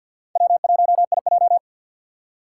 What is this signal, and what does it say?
Morse code, heard as a keyed tone, about 52 wpm, 700 Hz. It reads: W1IJ
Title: USA Call Signs - 2x Character Spacing - 50wpm